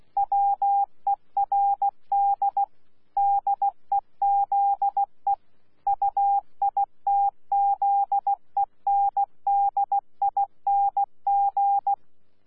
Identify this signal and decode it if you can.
Morse code, heard as a keyed tone, about 16 wpm, 800 Hz. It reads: WERD DEZE UITZENDING
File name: Z PI4VRZ / PI4VRZ 14-jun-2025